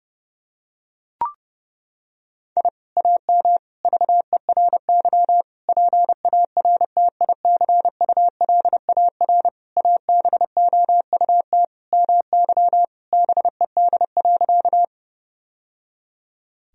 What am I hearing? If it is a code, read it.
Morse code, heard as a keyed tone, about 32 wpm, 700 Hz. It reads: I AM VERY PARTICULAR ABOUT MY BED.
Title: Sentences from Top 1000 Words - Review Part 4 - 30wpm